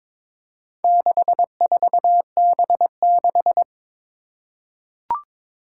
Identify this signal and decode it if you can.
Morse code, heard as a keyed tone, about 22 wpm, 700 Hz. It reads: 64B6
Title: Four Challenging Letter-Number Combinations 22wpm